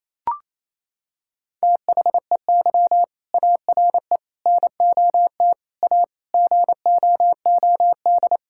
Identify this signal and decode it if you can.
Morse code, heard as a keyed tone, about 28 wpm, 700 Hz. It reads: THEY ARE NOT A GOOD
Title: Sentences from Top 1600 Words 28wpm